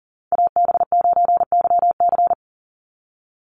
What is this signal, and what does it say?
Morse code, heard as a keyed tone, about 40 wpm, 700 Hz. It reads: AB9YC
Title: Call Signs Repeated 2x - 40wpm